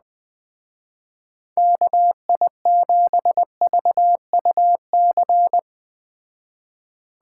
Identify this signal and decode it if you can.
Morse code, heard as a keyed tone, about 20 wpm, 700 Hz. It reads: KI7VUC